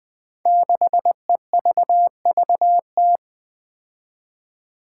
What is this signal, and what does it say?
Morse code, heard as a keyed tone, about 20 wpm, 700 Hz. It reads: 6EVVT
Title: Five Challenging Letter-Number Combinations 20wpm